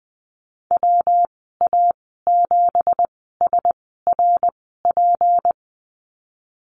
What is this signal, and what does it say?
Morse code, heard as a keyed tone, about 20 wpm, spaced as for 15 wpm, 700 Hz. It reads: WA7SRP
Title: 179 Lesson 29g - Slash - Call Signs 20wpm